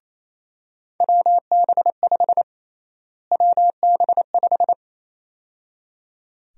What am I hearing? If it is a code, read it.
Morse code, heard as a keyed tone, about 28 wpm, 700 Hz. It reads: WB5 WB5
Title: USA Call Sign Prefixes - Repeated 2x - 28wpm